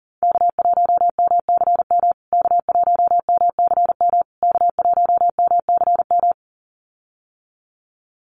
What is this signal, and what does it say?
Morse code, heard as a keyed tone, about 40 wpm, 700 Hz. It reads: K1MCM K1MCM K1MCM
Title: USA Call Signs Repeated 3x 40wpm